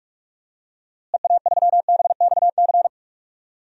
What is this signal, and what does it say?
Morse code, heard as a keyed tone, about 46 wpm, 700 Hz. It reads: EA3BXC
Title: DXCC-100 Call Signs - 6 Characters - Repeated 3x - 45wpm